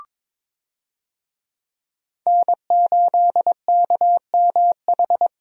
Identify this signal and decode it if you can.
Morse code, heard as a keyed tone, about 22 wpm, 700 Hz. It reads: N8KMH
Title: USA Call Signs - 5 Characters - Repeated 3x - 22wpm